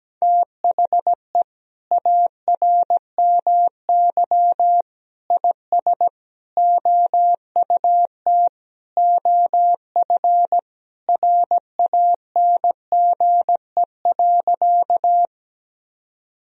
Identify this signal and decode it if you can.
Morse code, heard as a keyed tone, about 17 wpm, 700 Hz. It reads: THE ARMY IS OUT OF RANGE.
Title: Sentences from Top 1000 Words - Review Part 4 - 17wpm